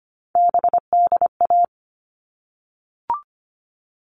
Morse code, heard as a keyed tone, about 25 wpm, 700 Hz. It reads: BDA
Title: All Cut Numbers - 3-Digit 25wpm